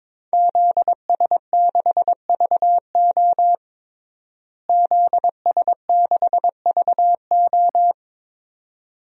Morse code, heard as a keyed tone, about 22 wpm, 700 Hz. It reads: ZS6VO ZS6VO
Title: DXCC-100 Call Signs - 5 Characters - Repeated 2x - 22wpm